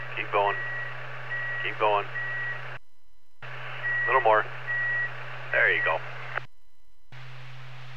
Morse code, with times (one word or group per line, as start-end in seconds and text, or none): none